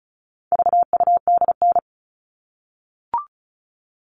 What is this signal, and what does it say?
Morse code, heard as a keyed tone, about 34 wpm, 700 Hz. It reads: VUDN
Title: All Cut Numbers - 4-Digit 35wpm